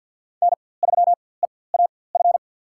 Morse code, heard as a keyed tone, about 53 wpm, spaced as for 24 wpm, 700 Hz. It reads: N3EAF
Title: USA Call Signs - 5 Characters - 4x Character Spacing - 50wpm